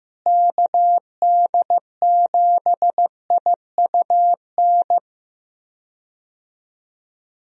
Morse code, heard as a keyed tone, about 15 wpm, 700 Hz. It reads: KD7IUN
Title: USA Call Signs - 6 Characters - Repeated 3x - 15wpm